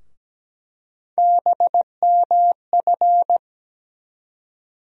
Morse code, heard as a keyed tone, about 17 wpm, 700 Hz. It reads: BMF
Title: USA Call Sign Postfixes - 3 Letters 17wpm